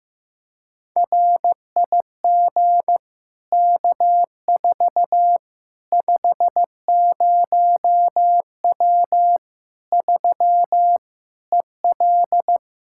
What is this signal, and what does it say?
Morse code, heard as a keyed tone, about 16 wpm, 700 Hz. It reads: RIG K4 50W 3 EL
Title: Practice QSO - Exchange 2 - Info - 15wpm